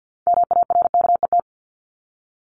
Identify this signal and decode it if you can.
Morse code, heard as a keyed tone, about 50 wpm, 700 Hz. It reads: MARKET
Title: Top 501-750 Words 50wpm